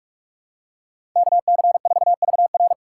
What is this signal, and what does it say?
Morse code, heard as a keyed tone, about 46 wpm, 700 Hz. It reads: KC4VR
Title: USA Call Signs - 5 Characters - 3x Character Spacing - 45wpm